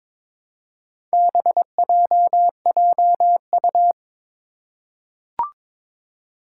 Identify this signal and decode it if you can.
Morse code, heard as a keyed tone, about 21 wpm, 700 Hz. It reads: BJJU
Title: Four Challenging Letter-Number Combinations 22wpm